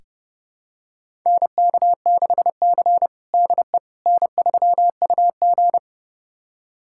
Morse code, heard as a keyed tone, about 30 wpm, 700 Hz. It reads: NK6C DE N3UG
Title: USA Callsign DE Callsign - 4 Characters - 30wpm